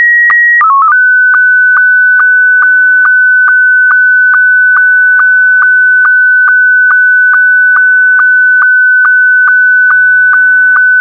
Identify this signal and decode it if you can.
SSTV (slow-scan television), Scottie 1.
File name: sstv-scottie1